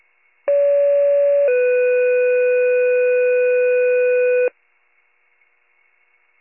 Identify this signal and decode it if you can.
Two-tone page, heard as a one-second tone, then a three-second tone, about 570 then 500 Hz. It is heard from 0.5 to 4.5 s.
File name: pager / 78 RESCUE / wcalert